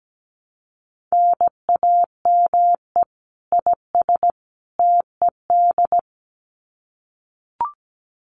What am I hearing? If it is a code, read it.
Morse code, heard as a keyed tone, about 17 wpm, 700 Hz. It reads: NAME IS TED
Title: Common QSO Phrases 17wpm